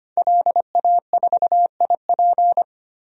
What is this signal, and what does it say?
Morse code, heard as a keyed tone, about 25 wpm, 700 Hz. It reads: LA4IP
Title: DXCC-100 Call Signs - 5 Characters - Repeated 2x - 25wpm